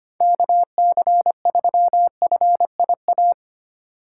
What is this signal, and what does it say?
Morse code, heard as a keyed tone, about 25 wpm, 700 Hz. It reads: KC3FIA
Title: USA Call Signs - 4x Character Spacing - 25wpm